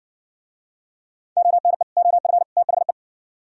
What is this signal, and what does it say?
Morse code, heard as a keyed tone, about 58 wpm, 700 Hz. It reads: ONE OF THE